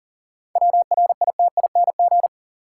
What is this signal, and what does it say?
Morse code, heard as a keyed tone, about 40 wpm, 700 Hz. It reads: WRITING
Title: Sentences from Top 800 Words - New Word Drill - 40wpm